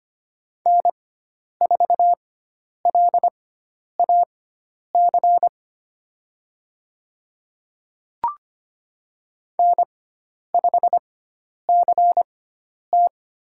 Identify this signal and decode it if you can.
Morse code, heard as a keyed tone, about 25 wpm, spaced as for 10 wpm, 700 Hz. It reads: N4LAC N5CT
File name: Lesson 16g - Letter c - Call Signs 25wpm